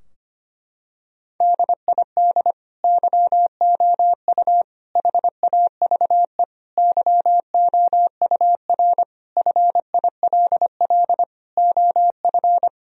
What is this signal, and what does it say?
Morse code, heard as a keyed tone, about 25 wpm, 700 Hz. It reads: DID YOU HAVE YOUR FILL OF
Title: Sentences from Top 1000 Words - Review Part 3 - 25wpm